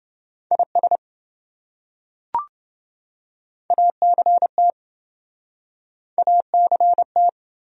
Morse code, heard as a keyed tone, about 26 wpm, 700 Hz. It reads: IS ACT ACT